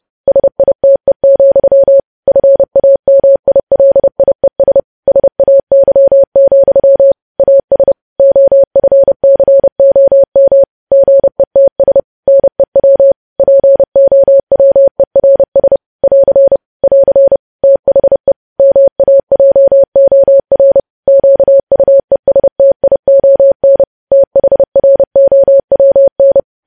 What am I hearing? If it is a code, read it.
Morse code, heard as a keyed tone, about 30 wpm, 550 Hz. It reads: SITE, FAMILIES SAY, AS OFCOM GETS NEW POWERS <AR> <AR> THE MAJOR QUESTION THROWN
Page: news headlines in morse code at 30 wpm